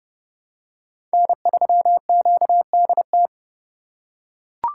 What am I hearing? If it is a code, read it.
Morse code, heard as a keyed tone, about 28 wpm, 700 Hz. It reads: N3QDT